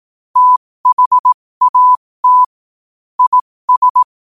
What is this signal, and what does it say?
Morse code, heard as a keyed tone, about 18 wpm, spaced as for 15 wpm, 1000 Hz. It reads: THAT IS
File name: qotdfiles / 15wpm